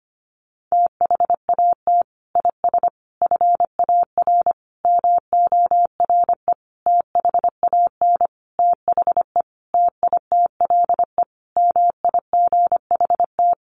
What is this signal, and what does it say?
Morse code, heard as a keyed tone, about 25 wpm, 700 Hz. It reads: THAT IS FAR MORE THAN THE TITLE MIGHT
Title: Sentences from Top 1200 Words 25wpm